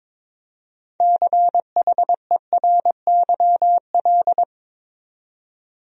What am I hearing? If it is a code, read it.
Morse code, heard as a keyed tone, about 22 wpm, 700 Hz. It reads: CHERYL
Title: USA Common Names - 22wpm